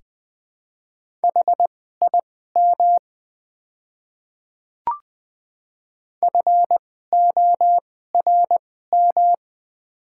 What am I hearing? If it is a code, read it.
Morse code, heard as a keyed tone, about 20 wpm, spaced as for 15 wpm, 690 Hz. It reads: HIM FORM
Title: 2x Character Spacing - Top 200 Words - 20wpm